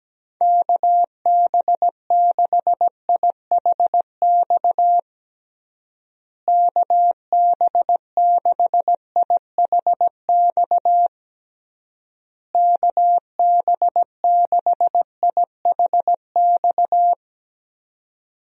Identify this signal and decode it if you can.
Morse code, heard as a keyed tone, about 17 wpm, 700 Hz. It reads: KB6IHX KB6IHX KB6IHX
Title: USA Call Signs - 6 Characters - Repeated 3x - 17wpm